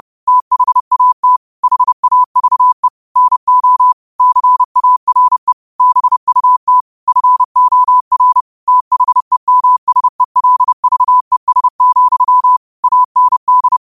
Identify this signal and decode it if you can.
Morse code, heard as a keyed tone, about 30 wpm, 1000 Hz. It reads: THAT HAVE NO CARE BUT FOR THEMSELVES, AND
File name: qotdfiles / 30wpm